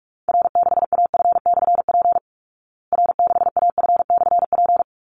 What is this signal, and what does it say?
Morse code, heard as a keyed tone, about 44 wpm, 700 Hz. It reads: R6AF/P R6AF/P
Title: Call Signs Repeated 3x - 45wpm